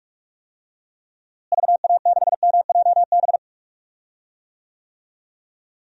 Morse code, heard as a keyed tone, about 41 wpm, 700 Hz. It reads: VA6MJB
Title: DXCC-100 Call Signs - 6 Characters - Repeated 2x - 45wpm